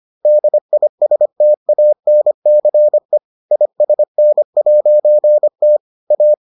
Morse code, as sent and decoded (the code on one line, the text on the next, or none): -.. .. ... - .- -. -.-. / .. ... -. .----. - / .-
DISTANCE ISN'T A